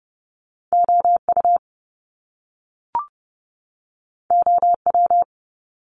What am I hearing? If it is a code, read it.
Morse code, heard as a keyed tone, about 30 wpm, 700 Hz. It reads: OU OW
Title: Top Two Letter Combinations in Words 30wpm